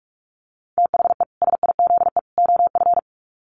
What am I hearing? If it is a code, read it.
Morse code, heard as a keyed tone, about 44 wpm, 700 Hz. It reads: THE SIZE OF
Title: Sets of 3 Words 45wpm